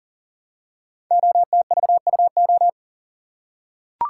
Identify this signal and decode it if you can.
Morse code, heard as a keyed tone, about 42 wpm, 700 Hz. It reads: OTVUO
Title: Code Groups - 5 Letter:Numbers - 40wpm